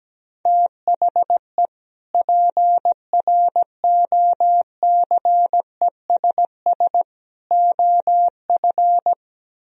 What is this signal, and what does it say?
Morse code, heard as a keyed tone, about 17 wpm, 700 Hz. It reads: THE PROCESS OF
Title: Speed-Racing - Sets of 3 Words 17wpm